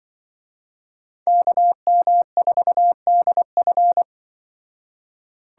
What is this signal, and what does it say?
Morse code, heard as a keyed tone, about 24 wpm, 700 Hz. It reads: KM4DF
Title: Speed Racing - USA 2x2 Call Signs 20wpm